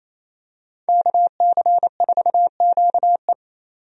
Morse code, heard as a keyed tone, about 27 wpm, 700 Hz. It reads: KC4QE
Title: USA Call Signs - 5 Characters - 28wpm